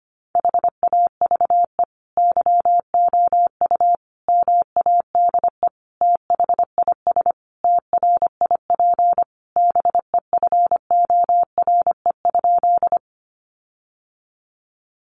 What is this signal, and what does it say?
Morse code, heard as a keyed tone, about 25 wpm, 700 Hz. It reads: HAVE YOU MADE THIS TRIP BEFORE?
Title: Sentences from Top 1300 Words 25wpm